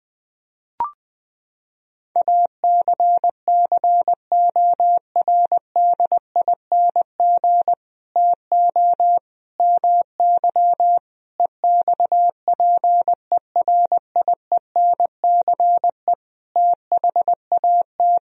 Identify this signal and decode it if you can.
Morse code, heard as a keyed tone, about 21 wpm, 700 Hz. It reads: ACCORDING TO MY EXPERIENCE THAT
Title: Sentences from Top 1000 Words - Review Part 4 - 20wpm